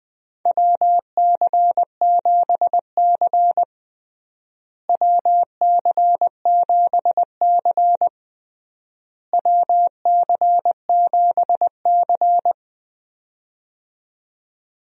Morse code, heard as a keyed tone, about 21 wpm, 700 Hz. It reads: WC7C WC7C WC7C
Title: USA Call Signs - 4 Characters - Repeated 3x - 20wpm